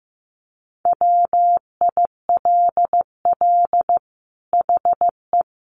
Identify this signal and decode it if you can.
Morse code, heard as a keyed tone, about 15 wpm, 700 Hz. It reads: WILL HE
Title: Sentences from Top 1300 Words 15wpm